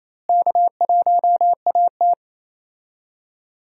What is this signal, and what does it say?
Morse code, heard as a keyed tone, about 27 wpm, 700 Hz. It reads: K1AT
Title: USA Call Signs - 4 Characters - Repeated 3x - 28wpm